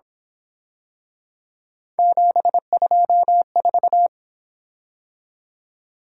Morse code, heard as a keyed tone, about 26 wpm, 700 Hz. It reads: 724
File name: Speed-Racing - Three Letter-Number Combinations - 22wpm